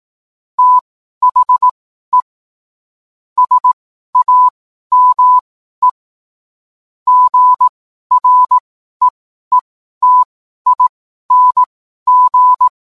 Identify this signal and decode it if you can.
Morse code, heard as a keyed tone, about 18 wpm, spaced as for 12 wpm, 1000 Hz. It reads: THE SAME GREETING
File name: cwblog / cw mp3 directory / mem morse 12